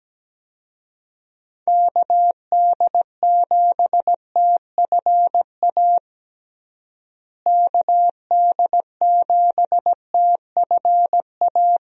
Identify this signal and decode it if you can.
Morse code, heard as a keyed tone, about 17 wpm, 700 Hz. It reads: KD7TFA KD7TFA
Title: USA Call Signs - 6 Characters - Repeated 3x - 17wpm